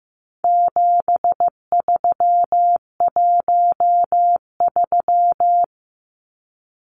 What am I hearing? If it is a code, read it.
Morse code, heard as a keyed tone, about 15 wpm, 700 Hz. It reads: 7313
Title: Numbers - 4-Digit 15wpm